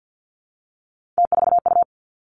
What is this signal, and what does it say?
Morse code, heard as a keyed tone, about 50 wpm, 700 Hz. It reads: T4U
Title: Three Challenging Letter-Number Combinations 50wpm